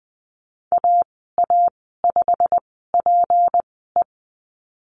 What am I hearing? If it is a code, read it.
Morse code, heard as a keyed tone, about 20 wpm, spaced as for 15 wpm, 700 Hz. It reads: AA5PE
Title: 231 Lesson 37g - Letter x - Call Signs 20wpm